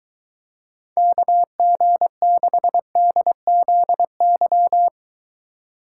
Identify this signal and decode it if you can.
Morse code, heard as a keyed tone, about 23 wpm, 700 Hz. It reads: KG6DZY